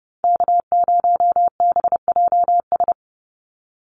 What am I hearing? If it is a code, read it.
Morse code, heard as a keyed tone, about 30 wpm, 700 Hz. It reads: K0BJS
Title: USA Call Signs 30wpm